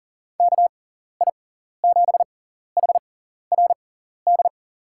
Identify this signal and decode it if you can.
Morse code, heard as a keyed tone, about 40 wpm, spaced as for 14 wpm, 700 Hz. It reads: KI7HRD
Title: USA Call Signs - 6x Character Spacing - 40wpm